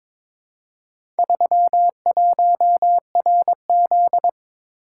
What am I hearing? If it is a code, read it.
Morse code, heard as a keyed tone, about 22 wpm, 700 Hz. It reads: 31RZ
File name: practice / 22wpm / Four Letter-Number Combinations - 22wpm